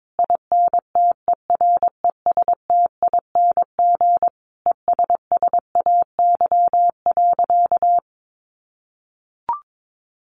Morse code, heard as a keyed tone, about 22 wpm, 700 Hz. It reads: INTERESTING ESSAY.
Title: Sentences from Top 1500 Words 22wpm